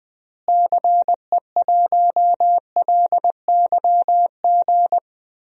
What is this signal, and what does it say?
Morse code, heard as a keyed tone, about 20 wpm, 700 Hz. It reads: CE1LYG
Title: DXCC-100 Call Signs - 6 Characters - Repeated 3x - 20wpm